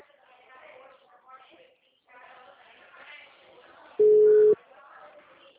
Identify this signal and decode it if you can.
Ringback tone (Europe, 425 Hz), with speech.